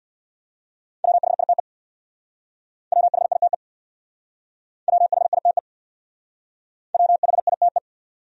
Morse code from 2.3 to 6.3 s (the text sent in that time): WHITE WHITE